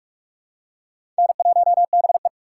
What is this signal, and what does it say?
Morse code, heard as a keyed tone, about 44 wpm, 700 Hz. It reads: N1BE
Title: USA Call Signs - 4 Characters - 6x Character Spacing - 45wpm